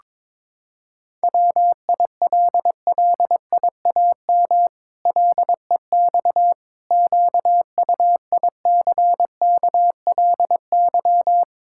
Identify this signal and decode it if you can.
Morse code, heard as a keyed tone, about 22 wpm, 700 Hz. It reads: WILLIAM LEX QUICKLY
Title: Pangram Challenge v2 - 22wpm